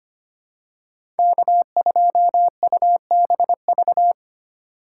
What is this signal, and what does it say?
Morse code, heard as a keyed tone, about 25 wpm, 700 Hz. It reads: K2UBV
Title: USA Call Signs - 5 Characters - 2x Character Spacing - 25wpm